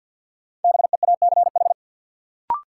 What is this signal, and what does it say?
Morse code, heard as a keyed tone, about 50 wpm, 700 Hz. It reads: BEACH
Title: Sentences from Top 1500 Words - New Word Drill - 50wpm